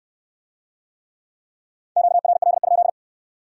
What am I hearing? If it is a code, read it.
Morse code, heard as a keyed tone, about 69 wpm, 700 Hz. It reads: QRL?